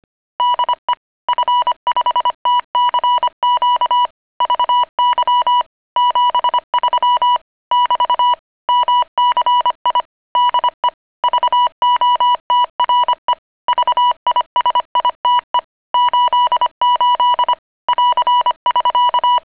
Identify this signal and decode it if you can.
Morse code, heard as a keyed tone, about 25 wpm, 980 Hz. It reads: DE F5TCQ VY 73 <BT> MCI DE VOTRE VISITE 88 <AR><SK>